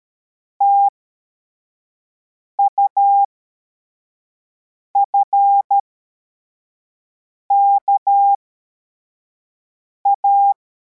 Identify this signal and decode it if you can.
Morse code, heard as a keyed tone, about 13 wpm, spaced as for 4 wpm, 800 Hz. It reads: TUFKA